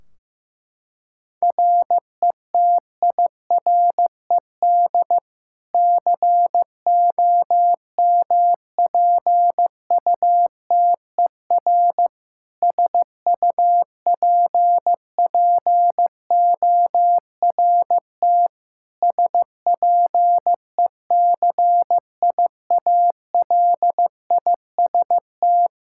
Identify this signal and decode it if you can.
Morse code, heard as a keyed tone, about 15 wpm, 700 Hz. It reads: RETIRED COMPUTER SUPPORT SPECIALIST
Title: Practice QSO - Exchange 3 - More Info - 15wpm